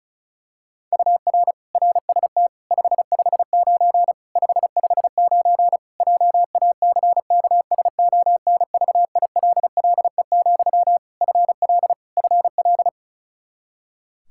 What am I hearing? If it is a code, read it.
Morse code, heard as a keyed tone, about 36 wpm, 700 Hz. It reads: UR RST 559 559 JACKSONVILLE, FL FL
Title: Practice QSO - Exchange 1 - Beginning - 35wpm